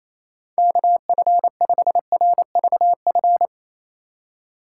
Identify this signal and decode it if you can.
Morse code, heard as a keyed tone, about 28 wpm, 700 Hz. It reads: KF5RVF
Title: USA Call Signs - 4x Character Spacing - 28wpm